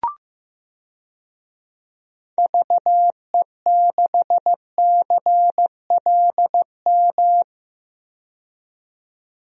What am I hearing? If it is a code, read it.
Morse code, heard as a keyed tone, about 15 wpm, 700 Hz. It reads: VE6CLM
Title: DXCC-100 Call Signs - 6 Characters - 15wpm